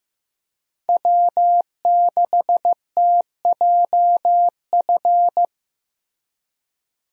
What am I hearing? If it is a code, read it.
Morse code, heard as a keyed tone, about 16 wpm, 700 Hz. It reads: W6TJF
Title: USA Call Signs - 5 Characters - Repeated 3x - 15wpm